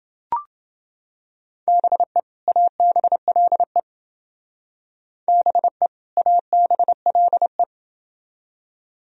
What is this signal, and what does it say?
Morse code, heard as a keyed tone, about 29 wpm, 700 Hz. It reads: BE ABLE BE ABLE